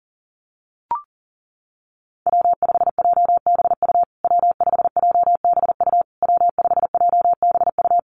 Morse code, heard as a keyed tone, about 40 wpm, 700 Hz. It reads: W5JBU W5JBU W5JBU
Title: USA Call Signs Repeated 3x 40wpm